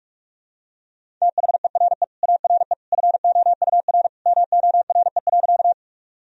Morse code, heard as a keyed tone, about 46 wpm, 700 Hz. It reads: THERE ARE FOUR MORE.
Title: Sentences from Top 1000 Words - Review Part 1 - 45wpm